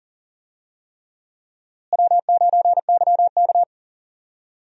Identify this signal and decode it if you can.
Morse code, heard as a keyed tone, about 40 wpm, 700 Hz. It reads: W9YK
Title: USA Call Signs - 4 Characters - Repeated 3x - 40wpm